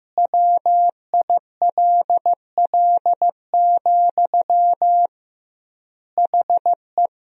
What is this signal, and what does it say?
Morse code, heard as a keyed tone, about 15 wpm, 700 Hz. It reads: WILL, HE
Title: 2x Word-Spacing - Sentences from Top 200 Words - 15wpm